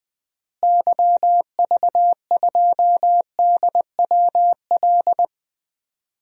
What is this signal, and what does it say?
Morse code, heard as a keyed tone, about 20 wpm, 700 Hz. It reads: YV2DWL
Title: DXCC-100 Call Signs - 6 Characters - Repeated 3x - 20wpm